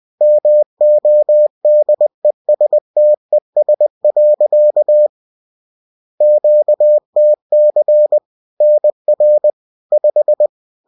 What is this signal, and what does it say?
Morse code, heard as a keyed tone, about 20 wpm, 600 Hz. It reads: MODESTES. QTC NR 5